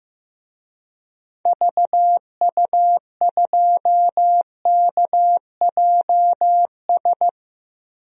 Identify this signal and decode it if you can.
Morse code, heard as a keyed tone, about 15 wpm, 700 Hz. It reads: VU2KJS
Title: DXCC-100 Call Signs - 6 Characters - Repeated 3x - 15wpm